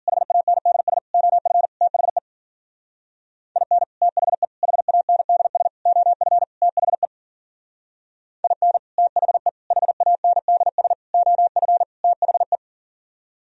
Morse code, heard as a keyed tone, about 51 wpm, 700 Hz. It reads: HANDS OF THE IN THE HANDS OF THE IN THE HANDS OF THE